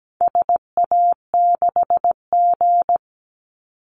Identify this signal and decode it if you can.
Morse code, heard as a keyed tone, about 17 wpm, 700 Hz. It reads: SA6G
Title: Call Signs Repeated 2x - 17wpm